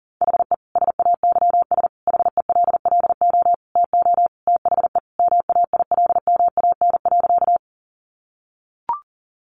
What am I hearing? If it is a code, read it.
Morse code, heard as a keyed tone, about 37 wpm, 700 Hz. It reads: HE SAYS HELLO TO THE MAILMAN.
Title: Additional Easy Phrases 40wpm